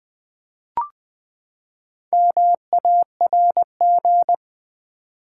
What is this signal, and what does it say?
Morse code, heard as a keyed tone, about 20 wpm, 700 Hz. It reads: MARG
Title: Build A Word - 4 Letters At A Time - 8-Letter Words - 20wpm